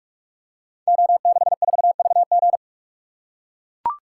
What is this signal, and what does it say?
Morse code, heard as a keyed tone, about 44 wpm, 700 Hz. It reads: O64VG
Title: Code Groups - 5 Letter:Numbers - 45wpm